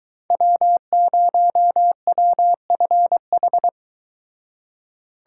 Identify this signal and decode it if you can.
Morse code, heard as a keyed tone, about 24 wpm, 700 Hz. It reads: W0WFH